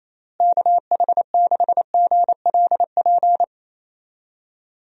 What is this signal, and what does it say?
Morse code, heard as a keyed tone, about 28 wpm, 700 Hz. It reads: KH6GLP
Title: DXCC-100 Call Signs - 6 Characters - Repeated 2x - 28wpm